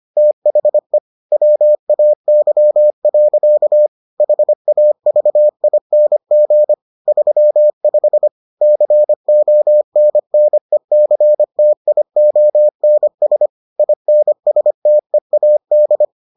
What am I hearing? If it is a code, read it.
Morse code, heard as a keyed tone, about 25 wpm, 600 Hz. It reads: THE WAY. HAVING 35 CONNECTIONS INSTEAD